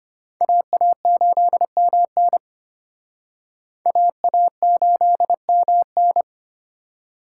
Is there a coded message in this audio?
Yes: Morse code from 0.4 to 6.2 s, heard as a keyed tone, about 34 wpm, 700 Hz. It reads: AA8MN AA8MN